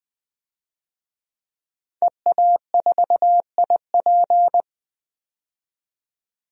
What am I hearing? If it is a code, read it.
Morse code, heard as a keyed tone, about 21 wpm, 700 Hz. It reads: EA4IP
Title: DXCC-100 Call Signs - 5 Characters - 20wpm